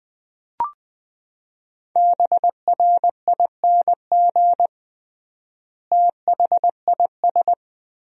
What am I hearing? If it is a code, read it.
Morse code, heard as a keyed tone, about 20 wpm, 700 Hz. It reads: BRING THIS
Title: 3x Word-Spacing - Sentences from Top 500 Words - 20wpm